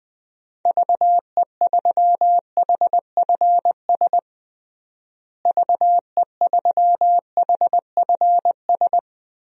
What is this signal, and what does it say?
Morse code, heard as a keyed tone, about 20 wpm, 700 Hz. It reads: VE3HFS VE3HFS
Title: DXCC-100 Call Signs - 6 Characters - Repeated 3x - 20wpm